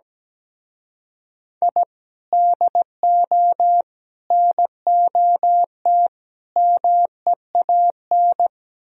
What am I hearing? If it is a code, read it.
Morse code, heard as a keyed tone, about 20 wpm, 700 Hz. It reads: I DO NOT MEAN
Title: Speed-Racing - Sets of 5 Words 17wpm